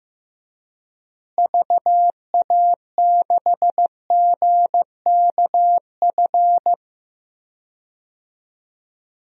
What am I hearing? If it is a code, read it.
Morse code, heard as a keyed tone, about 15 wpm, 700 Hz. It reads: VA6GKF
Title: DXCC-100 Call Signs - 6 Characters - Repeated 3x - 15wpm